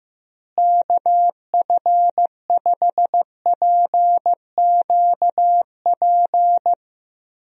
Morse code, heard as a keyed tone, about 15 wpm, 700 Hz. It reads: KF5PQP
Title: USA Call Signs - 6 Characters - Repeated 2x - 15wpm